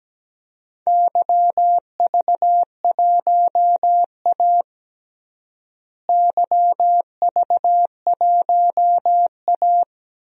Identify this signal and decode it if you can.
Morse code, heard as a keyed tone, about 17 wpm, 700 Hz. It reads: YV1A YV1A
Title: DXCC-100 Call Signs - 4 Characters - Repeated 3x - 17wpm